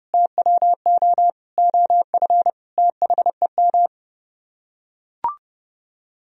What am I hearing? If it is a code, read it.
Morse code, heard as a keyed tone, about 30 wpm, 700 Hz. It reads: TWO OF THEM